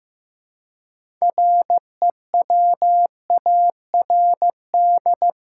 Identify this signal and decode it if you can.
Morse code, heard as a keyed tone, about 15 wpm, 700 Hz. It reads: REWARD
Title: Six Letter Words - 15wpm